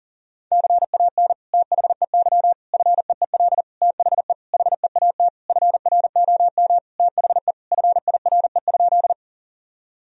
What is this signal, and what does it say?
Morse code, heard as a keyed tone, about 40 wpm, 700 Hz. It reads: CAN THEY FEEL THE HEAT FROM THE FIRE?
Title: Sentences from Top 1000 Words - Review Part 3 - 40wpm